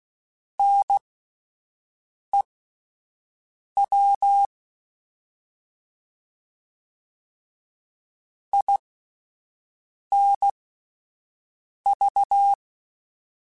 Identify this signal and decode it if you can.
Morse code, heard as a keyed tone, about 16 wpm, spaced as for 5 wpm, 800 Hz. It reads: NEW INV
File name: (spaced) / QSOs Spaced-mp3s / WD0FBT 1 @10 sp